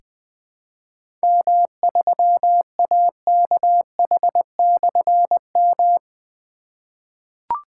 Morse code, heard as a keyed tone, about 20 wpm, 700 Hz. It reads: M3AKH/M
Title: Difficult Call Signs 20wpm